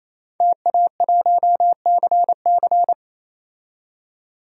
Morse code, heard as a keyed tone, about 28 wpm, 700 Hz. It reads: TA1CC
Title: DXCC-100 Call Signs - 5 Characters - 28wpm